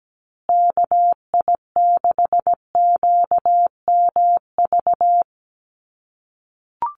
Morse code, heard as a keyed tone, about 17 wpm, 700 Hz. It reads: KI6QMV
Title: USA Call Signs Repeated 3x 17wpm